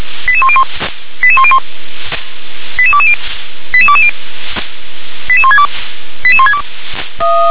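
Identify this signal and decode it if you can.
Five-tone selective call (ZVEI), heard as quick runs of five tones, tones of about 70 ms. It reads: 88101 88101 88220 88220 88162 88162
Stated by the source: Five-tone ZVEI selective call 88126, received by SDR.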